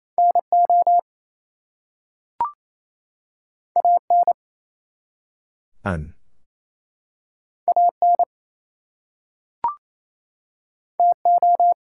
Morse code, heard as a keyed tone, about 28 wpm, 700 Hz. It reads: NO AN AN TO
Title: Two Letter Words - 28wpm